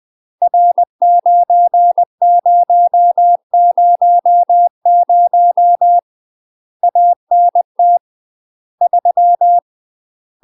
Morse code, heard as a keyed tone, about 20 wpm, 700 Hz. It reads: R9000 ANT 3
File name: UK MASTER QSOs TRAINING / 20WPM / UK MASTER QSOs 0041